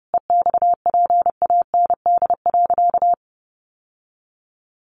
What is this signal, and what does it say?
Morse code, heard as a keyed tone, about 28 wpm, 700 Hz. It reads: EXPAND.
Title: Sentences from Top 1400 Words 30wpm